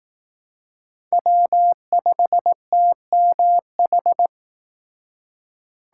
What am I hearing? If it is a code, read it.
Morse code, heard as a keyed tone, about 18 wpm, 700 Hz. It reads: W5TMH
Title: Speed Racing - USA Call Signs 15wpm